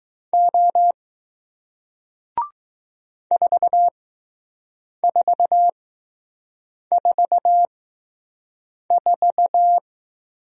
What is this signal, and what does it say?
Morse code, heard as a keyed tone, about 20 wpm, spaced as for 6 wpm, 700 Hz. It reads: O 4444